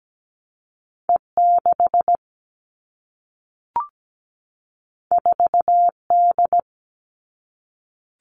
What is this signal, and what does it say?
Morse code, heard as a keyed tone, about 17 wpm, 700 Hz. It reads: E6 4D
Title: All Cut Numbers - 2-Digit 17wpm